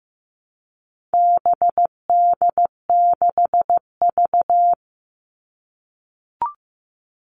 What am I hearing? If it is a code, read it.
Morse code, heard as a keyed tone, about 15 wpm, 700 Hz. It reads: BD6V
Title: All Cut Numbers - 4-Digit 15wpm